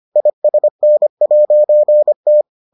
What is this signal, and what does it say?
Morse code, heard as a keyed tone, about 25 wpm, 600 Hz. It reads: ISN'T